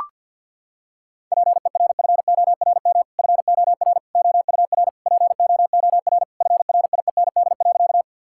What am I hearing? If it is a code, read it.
Morse code, heard as a keyed tone, about 50 wpm, 700 Hz. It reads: PERFORM FOR OUR POOR FRIEND.
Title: Sentences from Top 1000 Words - Review Part 5 - 50wpm